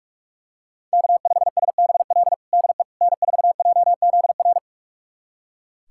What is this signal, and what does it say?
Morse code, heard as a keyed tone, about 46 wpm, 700 Hz. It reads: K5SBL DE N4JZR
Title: USA Callsign DE Callsign - 5 Characters - 45wpm